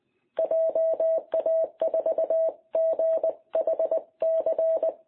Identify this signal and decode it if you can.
Morse code, heard as a keyed tone, about 19 wpm, 660 Hz. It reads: JA4GHC
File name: jh8cbh / 2021cw